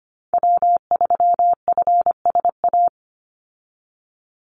Morse code, heard as a keyed tone, about 25 wpm, 700 Hz. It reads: W3FSA
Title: Call Signs - 25wpm